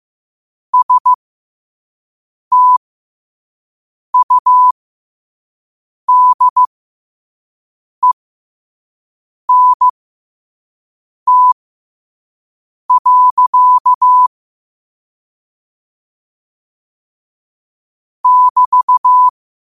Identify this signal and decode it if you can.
Morse code, heard as a keyed tone, about 15 wpm, spaced as for 5 wpm, 1000 Hz. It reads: STUDENT. <BT>
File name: qotdfiles / 5wpm